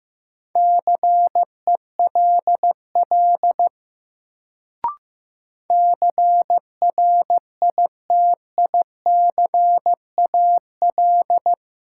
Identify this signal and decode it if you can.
Morse code, heard as a keyed tone, about 15 wpm, 700 Hz. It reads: CELL CRITICAL
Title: Sentences from Top 1100 Words - New Word Drill Alt - 15wpm